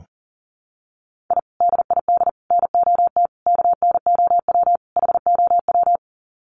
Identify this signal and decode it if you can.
Morse code, heard as a keyed tone, about 40 wpm, 700 Hz. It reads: I DID NOT KNOW HOW
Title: Sets of 5 Words Encore 40wpm